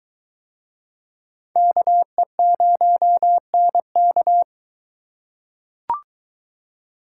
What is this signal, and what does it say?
Morse code, heard as a keyed tone, about 23 wpm, 700 Hz. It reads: KE0NK